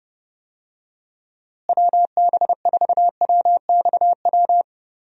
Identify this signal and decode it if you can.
Morse code, heard as a keyed tone, about 30 wpm, 700 Hz. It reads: WB4WXW
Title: USA Call Signs - 6 Characters - Repeated 2x - 30wpm